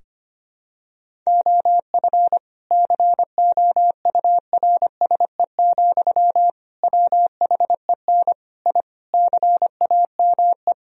Morse code, heard as a keyed tone, about 25 wpm, 700 Hz. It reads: OF COURSE, WHEN I CAME
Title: Sentences from Top 1000 Words - Review Part 3 - 25wpm